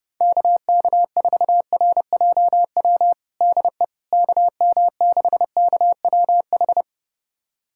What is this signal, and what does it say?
Morse code, heard as a keyed tone, about 30 wpm, 700 Hz. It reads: KK4RJW DE KM6KWH
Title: USA Callsign DE Callsign 30wpm